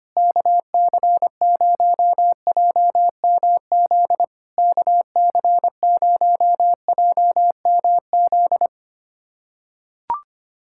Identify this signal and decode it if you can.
Morse code, heard as a keyed tone, about 25 wpm, 700 Hz. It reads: KC0JMZ KC0JMZ
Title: SOTA QSO Practice 25wpm